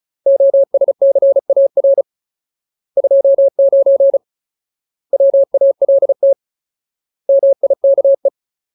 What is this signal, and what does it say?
Morse code, heard as a keyed tone, about 37 wpm, 550 Hz. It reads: OSCAR 29 WALT MIKE
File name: Ultimate Word List / 35WPM Ultimate